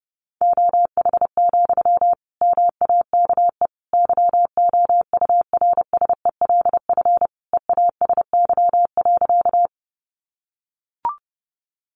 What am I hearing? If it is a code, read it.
Morse code, heard as a keyed tone, about 30 wpm, 700 Hz. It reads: OH, MAKE YOURSELF EASY.
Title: Sentences from Top 700 Words 30wpm